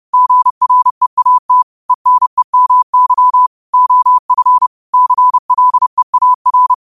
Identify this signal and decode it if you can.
Morse code, heard as a keyed tone, about 30 wpm, 1000 Hz. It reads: GREAT ENEMY OF CLEAR